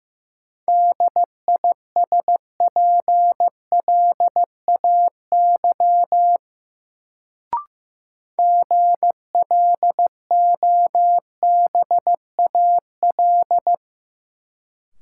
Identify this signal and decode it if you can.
Morse code, heard as a keyed tone, about 15 wpm, 700 Hz. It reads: DISPLAY GLOBAL
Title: Sentences from Top 1200 Words - New Word Drill - 15wpm